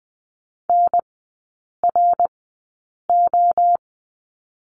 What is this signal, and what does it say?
Morse code, heard as a keyed tone, about 20 wpm, spaced as for 8 wpm, 700 Hz. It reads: NRO
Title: 053 Lesson 10e - Letter r - Three Characters 20wpm